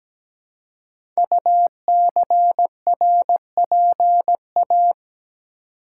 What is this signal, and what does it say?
Morse code, heard as a keyed tone, about 17 wpm, 700 Hz. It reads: UCRPA
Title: Code Groups - 5 Letter:Numbers - 17wpm